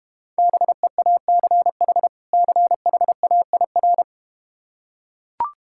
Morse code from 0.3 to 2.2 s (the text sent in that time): BEACH